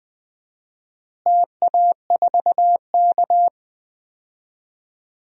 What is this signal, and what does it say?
Morse code, heard as a keyed tone, about 20 wpm, 700 Hz. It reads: TA4K
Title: DXCC-100 Call Signs - 4 Characters - Repeated 2x - 20wpm